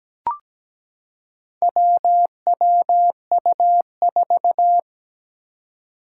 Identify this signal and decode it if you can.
Morse code, heard as a keyed tone, about 17 wpm, 700 Hz. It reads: WWU4
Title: Four Challenging Letter-Number Combinations 17wpm